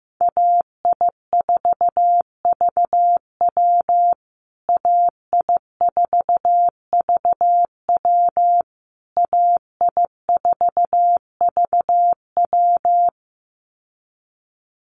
Morse code, heard as a keyed tone, about 15 wpm, 700 Hz. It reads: AI4VW AI4VW AI4VW
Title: USA Call Signs Repeated 3x 15wpm